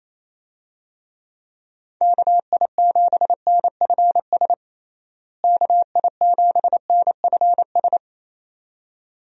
Morse code, heard as a keyed tone, about 27 wpm, 700 Hz. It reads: KI7NFS KI7NFS
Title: USA Call Signs - 6 Characters - Repeated 2x - 28wpm